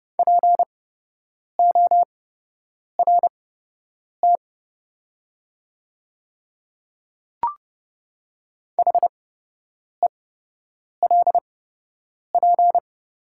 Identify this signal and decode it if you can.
Morse code, heard as a keyed tone, about 30 wpm, spaced as for 8 wpm, 700 Hz. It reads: PORT HELP
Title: 8x Character Spacing - Top 200 Words - 30wpm